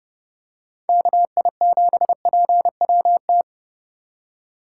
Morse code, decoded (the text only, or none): KI7PWT